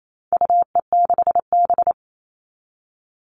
Morse code, heard as a keyed tone, about 28 wpm, 700 Hz. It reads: UE6B
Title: All Cut Numbers - 4-Digit 28wpm